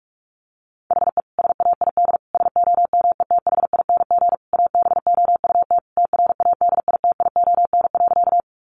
Morse code, heard as a keyed tone, about 44 wpm, 700 Hz. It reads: HE SAID SOMETHING ABOUT TRADITION.
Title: Sentences from Top 1500 Words 45wpm